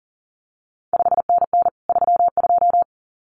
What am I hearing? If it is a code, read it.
Morse code, heard as a keyed tone, about 41 wpm, 700 Hz. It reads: 5NN 32
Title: CQ WW DX Contest Exchange Encore - 40wpm